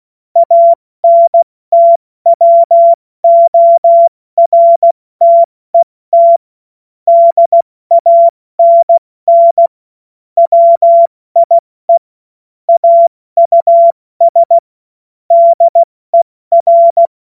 Morse code, heard as a keyed tone, about 16 wpm, spaced as for 14 wpm, 680 Hz. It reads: ANTWORTET DANN WIE AUS DER
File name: Morsekurs / Witze / Witz 16. August 70 F80